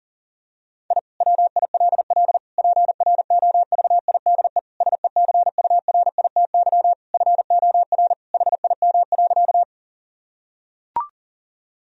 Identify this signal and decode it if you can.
Morse code, heard as a keyed tone, about 40 wpm, 700 Hz. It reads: I WILL PROVIDE SECURITY FOR HIM.
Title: Sentences from Top 1000 Words - Review Part 4 - 40wpm